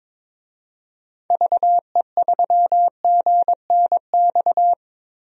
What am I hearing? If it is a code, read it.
Morse code, heard as a keyed tone, about 22 wpm, 700 Hz. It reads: VE3GNX
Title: DXCC-100 Call Signs - 6 Characters - Repeated 2x - 22wpm